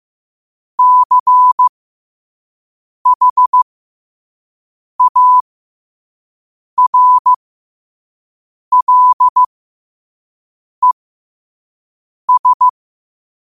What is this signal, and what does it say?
Morse code, heard as a keyed tone, about 15 wpm, spaced as for 5 wpm, 1000 Hz. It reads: CHARLES
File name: qotdfiles / 5wpm